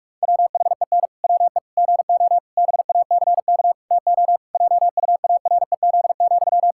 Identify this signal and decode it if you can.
Morse code, heard as a keyed tone, about 42 wpm, 700 Hz. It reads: WHEN WE GO BACK TO JUAREZ,
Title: Pangram Challenge v2 - 45wpm